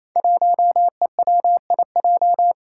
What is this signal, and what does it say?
Morse code, heard as a keyed tone, about 28 wpm, 700 Hz. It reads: 1EWIJ
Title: Five Challenging Letter-Number Combinations 28wpm